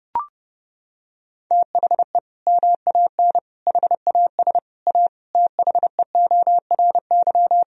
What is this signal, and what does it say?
Morse code, heard as a keyed tone, about 32 wpm, 700 Hz. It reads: THE MAN HAS A THEORY
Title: Sentences from Top 1000 Words - Review Part 3 - 30wpm